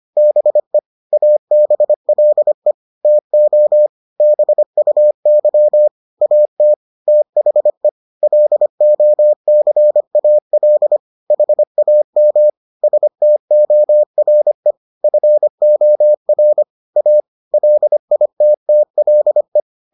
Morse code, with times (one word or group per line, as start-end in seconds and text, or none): 0.2-0.8 BE
1.1-2.7 ABLE
3.0-3.9 TO
4.2-5.9 BUY
6.2-6.7 AT
7.1-7.9 THE
8.2-11.0 LOCAL
11.3-12.5 HAM
12.8-14.7 STORE
15.0-16.6 FOR
17.0-17.2 A
17.5-19.6 LITTLE